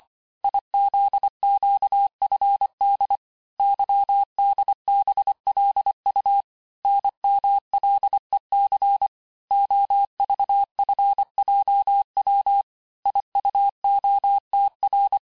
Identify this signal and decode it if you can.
Morse code, heard as a keyed tone, about 25 wpm, 800 Hz. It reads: IZQFD YDBLU NMLEC OVFJW IUOTR
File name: texte-chiffre-vitesse-23mpm